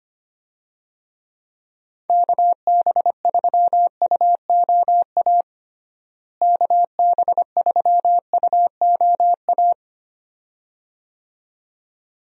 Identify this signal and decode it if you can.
Morse code, heard as a keyed tone, about 25 wpm, 700 Hz. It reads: KB3UOA KB3UOA
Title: USA Call Signs - 6 Characters - Repeated 2x - 25wpm